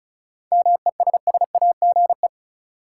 Morse code, heard as a keyed tone, about 34 wpm, 700 Hz. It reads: MESSAGE
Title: Sentences from Top 1100 Words - New Word Drill - 35wpm